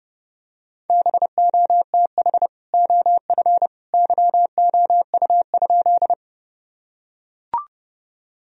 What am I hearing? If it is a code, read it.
Morse code, heard as a keyed tone, about 30 wpm, 700 Hz. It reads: BOTH OF YOU?
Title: Sentences from Top 1000 Words - Review Part 1 - 30wpm